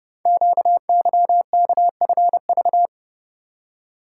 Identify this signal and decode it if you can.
Morse code, heard as a keyed tone, about 30 wpm, 700 Hz. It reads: QYKFV
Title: Code Groups - 5 Letters - 30wpm